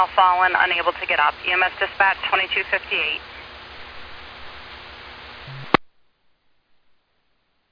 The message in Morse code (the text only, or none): none